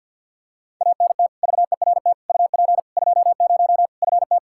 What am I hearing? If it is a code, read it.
Morse code, heard as a keyed tone, about 52 wpm, 700 Hz. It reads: ANT VERT UP 20 FT
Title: Practice QSO - Exchange 2 - Info - 50wpm